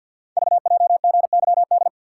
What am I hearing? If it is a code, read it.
Morse code, heard as a keyed tone, about 48 wpm, 700 Hz. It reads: VJGYD
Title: Code Groups - 5 Letters - 50wpm